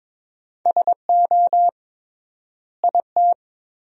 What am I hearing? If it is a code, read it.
Morse code, heard as a keyed tone, about 22 wpm, 700 Hz. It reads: SO IT
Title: 3x Word-Spacing - Sentences from Top 300 Words - 22wpm